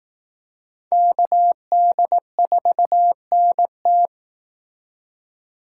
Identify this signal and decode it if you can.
Morse code, heard as a keyed tone, about 19 wpm, 700 Hz. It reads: KD4NT
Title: Speed Racing - USA 2x2 Call Signs 15wpm